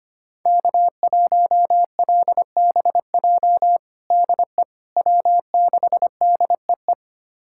Morse code, heard as a keyed tone, about 25 wpm, 700 Hz. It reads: K1LBJ DE W6DEE
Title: USA Callsign DE Callsign - 5 Characters - 25wpm